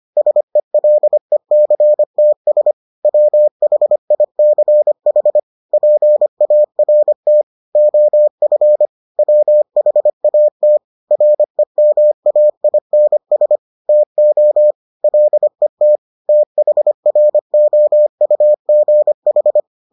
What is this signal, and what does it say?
Morse code, heard as a keyed tone, about 25 wpm, 600 Hz. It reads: SELECTS WHICH PART OF WHAT REMAINS TO LET THROUGH